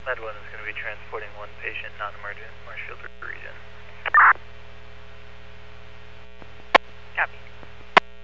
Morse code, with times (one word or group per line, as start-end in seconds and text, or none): none